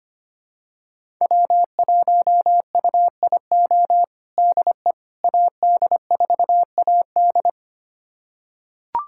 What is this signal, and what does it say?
Morse code, heard as a keyed tone, about 25 wpm, 700 Hz. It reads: W1UIO DE AD4AD
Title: USA Callsign DE Callsign - 5 Characters - 25wpm